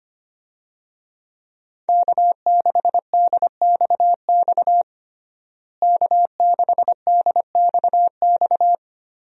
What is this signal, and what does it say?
Morse code, heard as a keyed tone, about 25 wpm, 700 Hz. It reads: K6DXX K6DXX
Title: USA Call Signs - 5 Characters - Repeated 3x - 25wpm